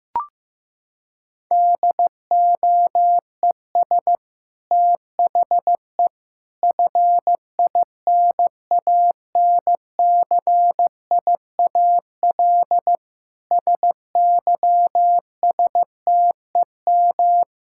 Morse code, heard as a keyed tone, about 15 wpm, 700 Hz. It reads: DOES THE FINANCIAL SYSTEM
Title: Sentences from Top 1000 Words - Review Part 5 - 15wpm